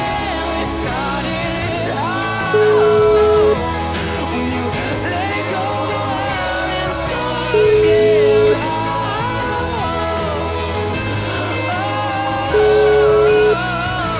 Ringback tone (Europe, 425 Hz), with noise.